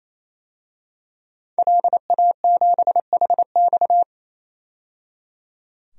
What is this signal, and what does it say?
Morse code, heard as a keyed tone, about 28 wpm, 700 Hz. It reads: LA7HX
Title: DXCC-100 Call Signs - 5 Characters - 28wpm